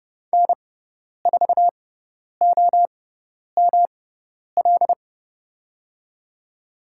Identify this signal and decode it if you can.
Morse code, heard as a keyed tone, about 30 wpm, spaced as for 10 wpm, 700 Hz. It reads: N4OML
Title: USA Call Signs - 5 Characters - 6x Character Spacing - 30wpm